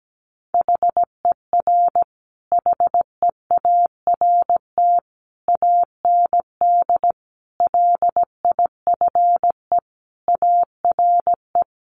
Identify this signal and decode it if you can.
Morse code, heard as a keyed tone, about 17 wpm, 700 Hz. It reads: HER HEART AND LIFE ARE
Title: Sentences from Top 1500 Words 17wpm